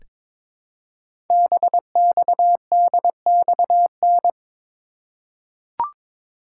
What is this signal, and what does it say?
Morse code, heard as a keyed tone, about 23 wpm, 700 Hz. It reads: BXDXN
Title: Code Groups - 5 Letter:Numbers - 22wpm